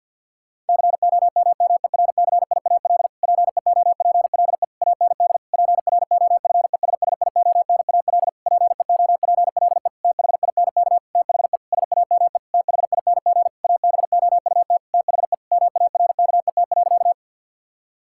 Morse code, heard as a keyed tone, about 49 wpm, 700 Hz. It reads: COMMERCIAL PEOPLE AND PROFESSIONAL PEOPLE THINK THE SAME THING ABOUT THE MARKET.
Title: Sentences from Top 1000 Words - Review Part 5 - 50wpm